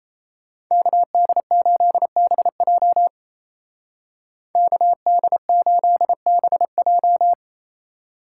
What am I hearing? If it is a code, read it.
Morse code, heard as a keyed tone, about 36 wpm, 700 Hz. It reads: KD8BJ KD8BJ